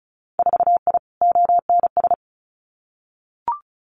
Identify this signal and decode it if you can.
Morse code, heard as a keyed tone, about 36 wpm, 700 Hz. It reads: 4I ONS
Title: Winter Field Day Exchanges Encore 35wpm